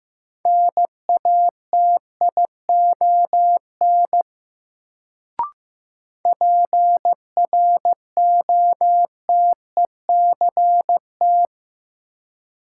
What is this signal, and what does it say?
Morse code, heard as a keyed tone, about 15 wpm, 700 Hz. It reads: NATION PROTECT
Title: Sentences from Top 900 Words - New Word Drill - 15wpm